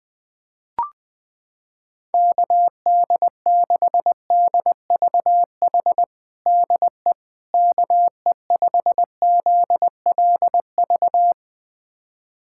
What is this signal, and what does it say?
Morse code, heard as a keyed tone, about 20 wpm, 700 Hz. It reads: KD6DVH DE KE5ZLV
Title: USA Callsign DE Callsign 20wpm